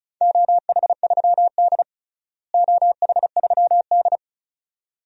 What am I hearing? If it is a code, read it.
Morse code, heard as a keyed tone, about 34 wpm, 700 Hz. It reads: OH3D OH3D
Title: DXCC-100 Call Signs - 4 Characters - Repeated 2x - 35wpm